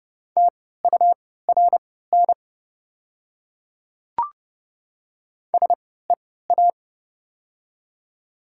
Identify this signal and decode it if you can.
Morse code, heard as a keyed tone, about 30 wpm, spaced as for 17 wpm, 700 Hz. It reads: TURN SEA
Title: 3x Character Spacing - Top 300 Words - 30wpm